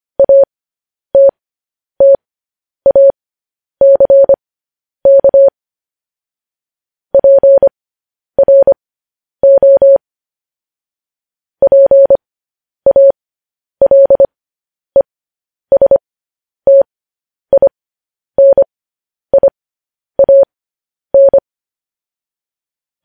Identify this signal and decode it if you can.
Morse code, heard as a keyed tone, about 25 wpm, spaced as for 10 wpm, 550 Hz. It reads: ATTACK PRO PALESTINIAN